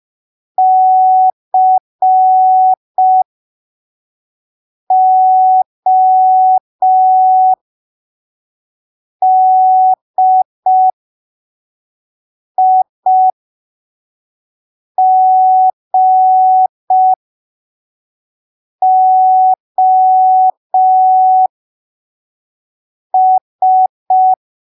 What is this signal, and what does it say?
Morse code, heard as a keyed tone, about 5 wpm, spaced as for 3 wpm, 750 Hz. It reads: CODIGOS
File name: arquivos / texto / texto 04 5ppm